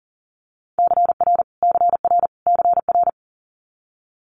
Morse code, heard as a keyed tone, about 38 wpm, 700 Hz. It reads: CR CR CR
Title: Multi-letter Phonemes Repeated 3x 40wpm